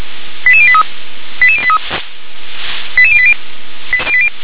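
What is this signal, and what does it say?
Five-tone selective call (ZVEI), heard as quick runs of five tones, tones of about 70 ms. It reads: 88083 88083 88080
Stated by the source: ZVEI five-tone selective call received on SDR.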